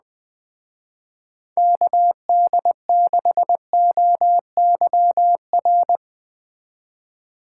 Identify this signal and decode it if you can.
Morse code, heard as a keyed tone, about 21 wpm, 700 Hz. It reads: KD6OYR